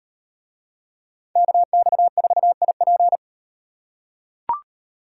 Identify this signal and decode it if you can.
Morse code, heard as a keyed tone, about 38 wpm, 700 Hz. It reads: KX4IP